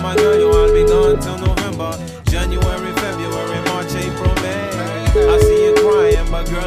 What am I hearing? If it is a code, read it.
Ringback tone (Europe, 425 Hz), with noise.